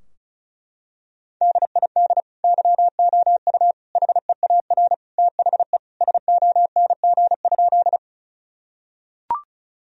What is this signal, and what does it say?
Morse code, heard as a keyed tone, about 36 wpm, 700 Hz. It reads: DID YOU HEAR THE SONG?
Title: Sentences from Top 1000 Words - Review Part 2 - 35wpm